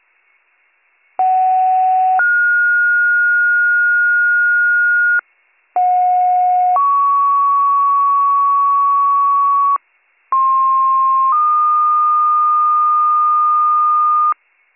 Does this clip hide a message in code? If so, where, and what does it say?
Yes: two-tone page from 1.2 to 14.3 s, heard as a one-second tone, then a three-second tone, about 750 then 1430 Hz, about 730 then 1120 Hz, about 1060 then 1220 Hz.